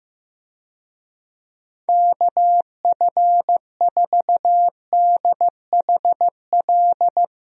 Morse code, heard as a keyed tone, about 15 wpm, 700 Hz. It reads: KF4DHL
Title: USA Call Signs - 6 Characters - Repeated 2x - 15wpm